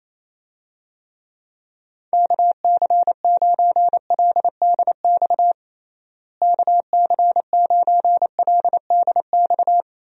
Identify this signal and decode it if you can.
Morse code, heard as a keyed tone, about 28 wpm, 700 Hz. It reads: KC9LDX KC9LDX
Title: USA Call Signs - 6 Characters - Repeated 3x - 28wpm